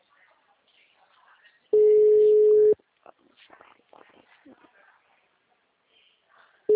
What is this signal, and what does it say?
Ringback tone (Europe, 425 Hz), with speech.
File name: out- -2011- - - .23681